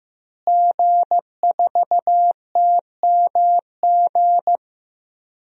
Morse code, heard as a keyed tone, about 15 wpm, 700 Hz. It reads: G4TMG
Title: DXCC-100 Call Signs - 5 Characters - Repeated 2x - 15wpm